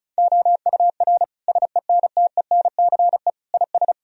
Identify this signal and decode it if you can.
Morse code, heard as a keyed tone, about 34 wpm, 700 Hz. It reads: OUR SENTENCE IS
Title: Sentences from Top 1000 Words - Review Part 1 - 35wpm